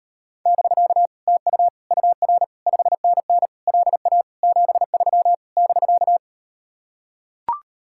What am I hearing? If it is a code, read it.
Morse code, heard as a keyed tone, about 39 wpm, 700 Hz. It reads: <BK> TU UR 5NN LA 73 <BK>